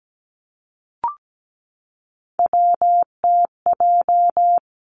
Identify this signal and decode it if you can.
Morse code, heard as a keyed tone, about 17 wpm, 700 Hz. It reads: WTJ
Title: Three Challenging Letter-Number Combinations 17wpm